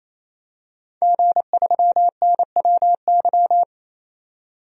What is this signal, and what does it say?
Morse code, heard as a keyed tone, about 28 wpm, 700 Hz. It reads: G3NWY
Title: DXCC-100 Call Signs - 5 Characters - 28wpm